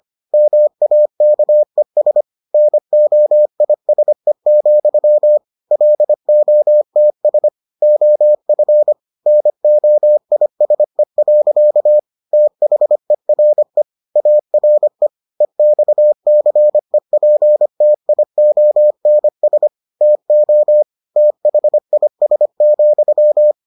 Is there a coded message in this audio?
Yes: Morse code from 0.3 to 23.5 s, heard as a keyed tone, about 25 wpm, 600 Hz. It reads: MAKES NOISE, LOTS OF NOISE. THERE ARE EXCEPTIONS TO THIS,